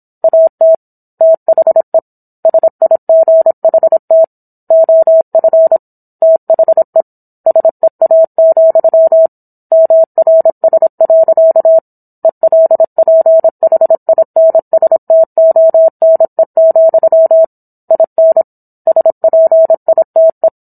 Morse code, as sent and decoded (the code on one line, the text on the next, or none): .- - / - .... / ... .. --. .... - / --- ..-. / - .... / ... .- --..-- / -- .-. ... .-.-.- / . .-.. .--. .... .. -. ... - --- -. --..-- / .. -. / ... .--. .. - .
AT THE SIGHT OF THE SEA, MRS. ELPHINSTONE, IN SPITE